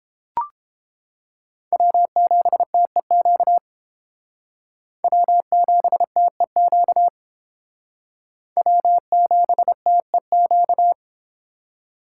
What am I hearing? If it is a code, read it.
Morse code, heard as a keyed tone, about 31 wpm, 700 Hz. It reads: W7TEQ W7TEQ W7TEQ